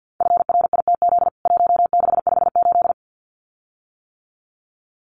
Morse code, heard as a keyed tone, about 48 wpm, 700 Hz. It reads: FRITZ 1658
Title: CWT Exchange 50wpm